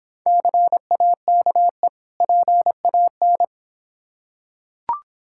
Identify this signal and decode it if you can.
Morse code, heard as a keyed tone, about 26 wpm, 700 Hz. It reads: CAKE PAN